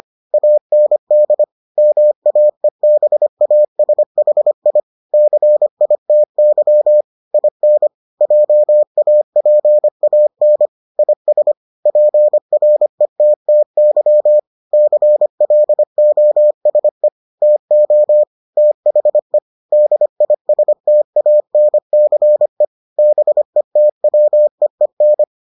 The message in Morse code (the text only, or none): AND MAEBASHI CITY IN JAPAN IS PRETTY CLOSE TO THE DISTANCE BETWEEN